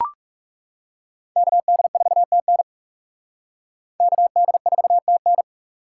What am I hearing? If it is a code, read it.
Morse code, heard as a keyed tone, about 38 wpm, 700 Hz. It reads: KD4TN KD4TN